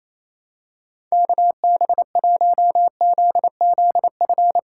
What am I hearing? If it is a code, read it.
Morse code, heard as a keyed tone, about 28 wpm, 700 Hz. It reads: KB1ZZF
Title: USA Call Signs - 6 Characters - 3x Character Spacing - 28wpm